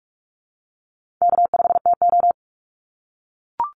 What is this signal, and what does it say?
Morse code, heard as a keyed tone, about 44 wpm, 700 Hz. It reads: K5TO
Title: Call Signs - 45wpm